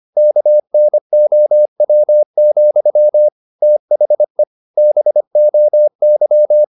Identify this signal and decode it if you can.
Morse code, heard as a keyed tone, about 25 wpm, 600 Hz. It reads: KNOW, THE BOY
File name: the old man and the sea part 1 25wpm